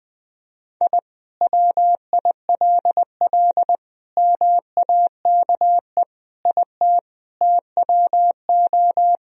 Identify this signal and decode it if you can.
Morse code, heard as a keyed tone, about 21 wpm, 700 Hz. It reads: I WILL MAKE IT TWO
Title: Sentences from Top 1000 Words - Review Part 2 - 20wpm